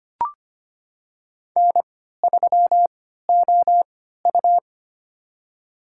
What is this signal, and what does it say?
Morse code, heard as a keyed tone, about 25 wpm, spaced as for 14 wpm, 700 Hz. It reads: N3OU
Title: USA Call Signs - 4 Characters - 3x Character Spacing - 25wpm